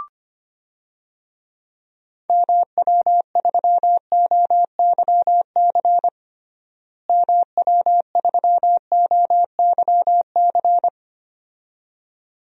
Morse code, heard as a keyed tone, about 25 wpm, 700 Hz. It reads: MW3OYC MW3OYC
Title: DXCC-100 Call Signs - 6 Characters - Repeated 2x - 25wpm